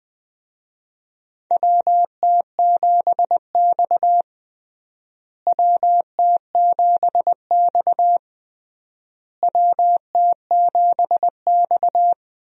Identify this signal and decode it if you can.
Morse code, heard as a keyed tone, about 20 wpm, 700 Hz. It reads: WT7X WT7X WT7X
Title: USA Call Signs - 4 Characters - Repeated 3x - 20wpm